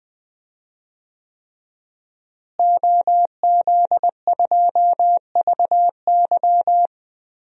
Morse code, heard as a keyed tone, about 20 wpm, 700 Hz. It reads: OZ2VY
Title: DXCC-100 Call Signs - 5 Characters - Repeated 3x - 20wpm